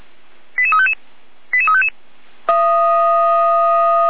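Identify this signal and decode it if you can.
Five-tone selective call (ZVEI), heard as quick runs of five tones, tones of about 80 ms. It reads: 88355 88355
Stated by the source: ZVEI five-tone selective call received by SDR, recorded on the Feuerwehr Orlamünde audio server.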